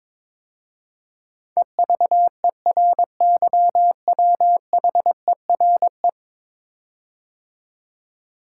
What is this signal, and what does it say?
Morse code, heard as a keyed tone, about 22 wpm, 700 Hz. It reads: EVERYWHERE
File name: 10 Letter Words - First 3 Letters Given - 22wpm